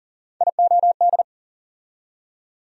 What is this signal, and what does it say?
Morse code, heard as a keyed tone, about 40 wpm, 700 Hz. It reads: IOD
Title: USA Call Sign Postfixes - 3 Letters 40wpm